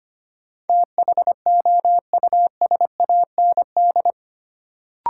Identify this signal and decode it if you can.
Morse code, heard as a keyed tone, about 25 wpm, 700 Hz. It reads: THOUSAND
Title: Sentences from Top 600 Words - New Word Drill - 25wpm